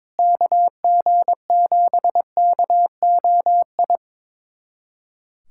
Morse code, heard as a keyed tone, about 22 wpm, 700 Hz. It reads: KG7KOI
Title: USA Callsign DE Callsign - 6 Characters - 22wpm